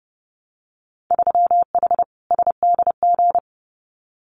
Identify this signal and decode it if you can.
Morse code, heard as a keyed tone, about 30 wpm, 700 Hz. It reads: 3H SDG
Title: Winter Field Day Exchanges Encore 30wpm